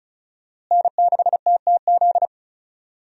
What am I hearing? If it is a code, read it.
Morse code, heard as a keyed tone, about 34 wpm, 700 Hz. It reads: N6TTZ
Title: USA Call Signs - 5 Characters - Repeated 3x - 35wpm